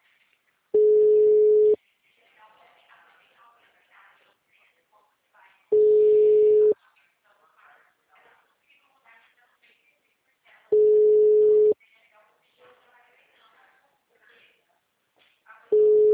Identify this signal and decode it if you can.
Ringback tone (Europe, 425 Hz), with speech.